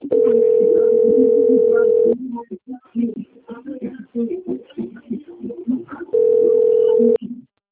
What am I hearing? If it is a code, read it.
Ringback tone (North America), 440 + 480 Hz, with speech.
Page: Piracicaba - SP